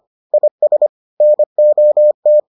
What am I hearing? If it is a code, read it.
Morse code, heard as a keyed tone, about 25 wpm, 600 Hz. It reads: IS NOT